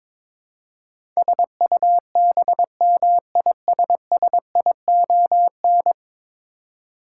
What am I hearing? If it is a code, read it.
Morse code, heard as a keyed tone, about 23 wpm, 700 Hz. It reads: SUBMISSION
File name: practice / 22wpm / 10 Letter Words - First 3 Letters Given - 22wpm